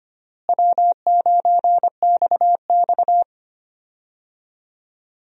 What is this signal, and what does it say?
Morse code, heard as a keyed tone, about 25 wpm, 700 Hz. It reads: W9XX
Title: USA Call Signs - 4 Characters - Repeated 3x - 25wpm